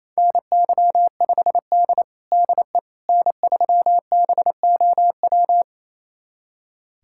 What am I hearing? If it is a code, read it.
Morse code, heard as a keyed tone, about 28 wpm, 700 Hz. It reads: NY5D DE N3BOW
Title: USA Callsign DE Callsign 28wpm